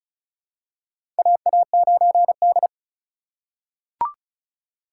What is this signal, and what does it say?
Morse code, heard as a keyed tone, about 36 wpm, 700 Hz. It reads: AA9D
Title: USA Call Signs - 4 Characters - 2x Character Spacing - 35wpm